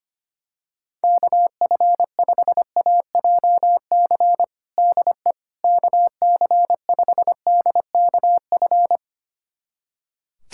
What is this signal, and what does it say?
Morse code, heard as a keyed tone, about 25 wpm, 700 Hz. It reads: KF5AJC DE KC5DKF
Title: USA Callsign DE Callsign - 6 Characters - 25wpm